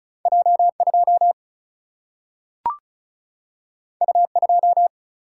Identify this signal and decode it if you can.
Morse code, heard as a keyed tone, about 36 wpm, 700 Hz. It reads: J 2 U 2
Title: Single-Letter Single-Number 35wpm